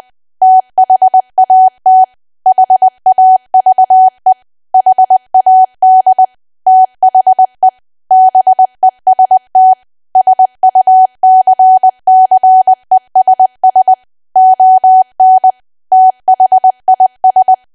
Morse code, heard as a keyed tone, about 20 wpm, 750 Hz. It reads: THAT HAVE HAD THE BEST SUCCESS ON THIS
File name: cwa / W1AW-Level3 / 20WPM